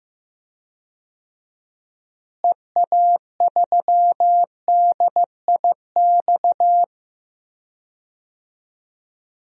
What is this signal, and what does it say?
Morse code, heard as a keyed tone, about 15 wpm, 700 Hz. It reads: EA3DIX
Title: DXCC-100 Call Signs - 6 Characters - Repeated 3x - 15wpm